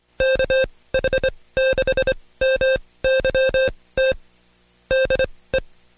Morse code, heard as a keyed tone, about 24 wpm, spaced as for 18 wpm, 530 Hz. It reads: KH6MYT DE
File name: Code Quick / AudioDownld2 / Seg135-Stepper8 17WPM